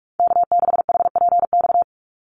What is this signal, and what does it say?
Morse code, heard as a keyed tone, about 44 wpm, 700 Hz. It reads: K6HPX
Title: NA SOTA Activator Callsigns 45wpm